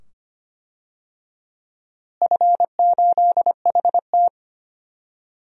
Morse code, heard as a keyed tone, about 25 wpm, 700 Hz. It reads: F8HT
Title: DXCC-100 Call Signs - 4 Characters - Repeated 2x - 25wpm